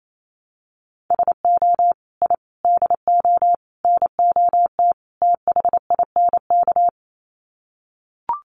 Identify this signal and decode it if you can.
Morse code, heard as a keyed tone, about 28 wpm, 700 Hz. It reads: SO I DO NOT THINK
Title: Sets of 5 Words 28wpm